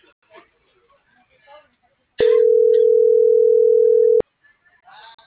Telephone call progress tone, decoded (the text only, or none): ringback tone